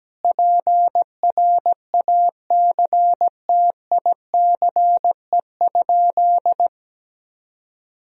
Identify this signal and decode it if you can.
Morse code, heard as a keyed tone, about 17 wpm, 700 Hz. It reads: PRACTICE?
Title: Sentences from Top 1000 Words - Review Part 4 - 17wpm